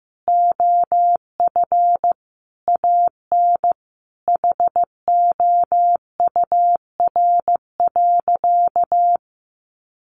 Morse code, heard as a keyed tone, about 15 wpm, 700 Hz. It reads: OF AN HOUR.
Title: Sentences from Top 1300 Words 15wpm